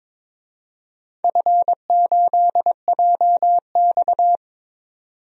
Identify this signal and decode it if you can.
Morse code, heard as a keyed tone, about 22 wpm, 700 Hz. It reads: F8JX
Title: DXCC-100 Call Signs - 4 Characters - 22wpm